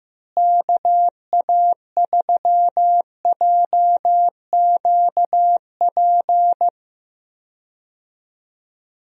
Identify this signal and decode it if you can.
Morse code, heard as a keyed tone, about 15 wpm, 700 Hz. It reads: KA3JQP
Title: USA Call Signs - 6 Characters - Repeated 3x - 15wpm